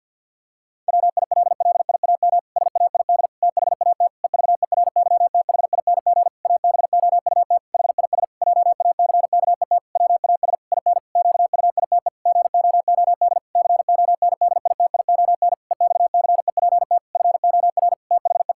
Morse code, heard as a keyed tone, about 50 wpm, 700 Hz. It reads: WILLIAM SAID THAT EVERYTHING ABOUT HIS JACKET WAS IN QUITE GOOD CONDITION EXCEPT FOR THE
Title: Pangram Challenge v2 - 50wpm